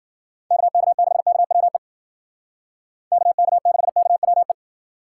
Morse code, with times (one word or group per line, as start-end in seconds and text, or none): 0.5-1.8 KK6KWE
3.1-4.5 KK6KWE